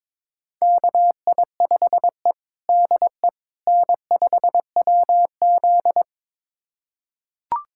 Morse code, heard as a keyed tone, about 22 wpm, 700 Hz. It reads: KI5E DE N5WZ
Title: USA Callsign DE Callsign - 4 Characters - 22wpm